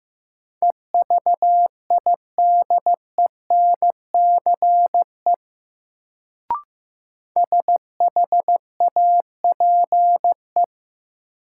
Morse code, heard as a keyed tone, about 15 wpm, 700 Hz. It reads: EVIDENCE SHAPE
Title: Sets of 5 Words Encore - New Word Drill - 15wpm